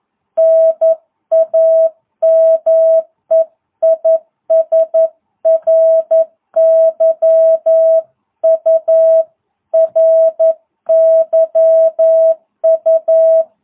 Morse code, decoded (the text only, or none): NAME IS RYU RYU